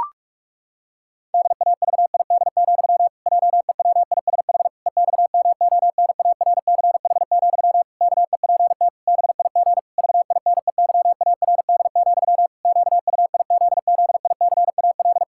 Morse code, heard as a keyed tone, about 43 wpm, 700 Hz. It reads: DAVID, JEWISH EXMONARCH, KEPT BIG VINEYARD, QUIZZICAL
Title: Pangram Challenge v2 - 45wpm